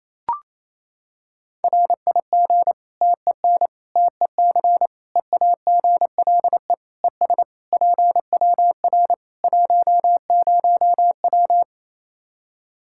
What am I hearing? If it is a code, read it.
Morse code, heard as a keyed tone, about 28 wpm, 700 Hz. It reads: RIG TEN TEC EAGLE ES PWR 10W
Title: K1USN SST QSO Practice 28wpm